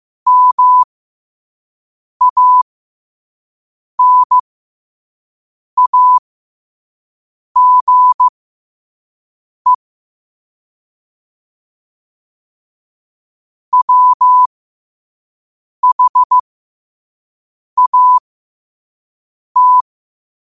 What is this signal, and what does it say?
Morse code, heard as a keyed tone, about 15 wpm, spaced as for 5 wpm, 1000 Hz. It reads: MANAGE WHAT
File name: qotdfiles / 5wpm